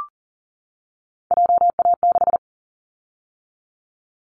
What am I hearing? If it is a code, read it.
Morse code, heard as a keyed tone, about 40 wpm, 700 Hz. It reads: JA6
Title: Callsign Prefixes for Top 25 DXCC - 40wpm